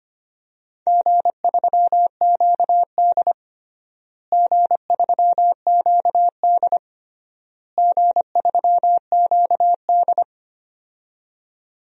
Morse code, heard as a keyed tone, about 25 wpm, 700 Hz. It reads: G3QD G3QD G3QD
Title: DXCC-100 Call Signs - 4 Characters - Repeated 3x - 25wpm